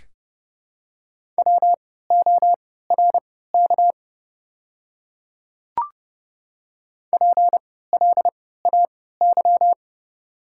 Morse code, heard as a keyed tone, about 30 wpm, spaced as for 17 wpm, 700 Hz. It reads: WORK PLAY
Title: 3x Character Spacing - Top 300 Words - 30wpm